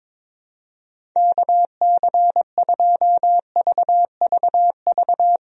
Morse code, heard as a keyed tone, about 22 wpm, 700 Hz. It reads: KC2VVV
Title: USA Call Signs - 6 Characters - Repeated 2x - 22wpm